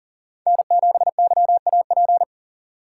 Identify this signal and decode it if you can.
Morse code, heard as a keyed tone, about 40 wpm, 700 Hz. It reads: N7YAP
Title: USA Call Signs - 5 Characters - 4x Character Spacing - 40wpm